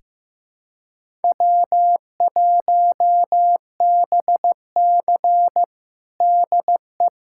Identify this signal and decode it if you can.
Morse code, heard as a keyed tone, about 15 wpm, 700 Hz. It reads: W1BC DE
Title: USA Callsign DE Callsign - 4 Characters - 15wpm